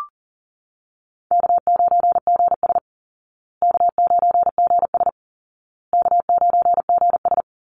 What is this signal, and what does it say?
Morse code, heard as a keyed tone, about 40 wpm, 700 Hz. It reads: K9GS K9GS K9GS
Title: Call Signs Repeated 2x - 40wpm